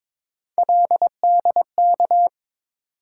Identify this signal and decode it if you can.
Morse code, heard as a keyed tone, about 22 wpm, 700 Hz. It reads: LDK
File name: practice / 22wpm / Three Letter-Number Combinations - 22wpm